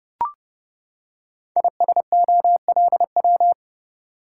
Morse code, heard as a keyed tone, about 30 wpm, 700 Hz. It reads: ISOLW
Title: Code Groups - 5 Letter:Numbers - 30wpm